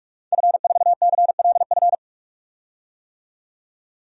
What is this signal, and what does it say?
Morse code, heard as a keyed tone, about 46 wpm, 700 Hz. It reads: F4CLF
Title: DXCC-100 Call Signs - 5 Characters - 45wpm